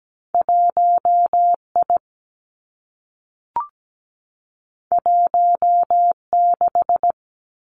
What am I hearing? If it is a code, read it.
Morse code, heard as a keyed tone, about 17 wpm, 700 Hz. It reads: 1I 16
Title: Two Challenging Letter-Number Combinations 17wpm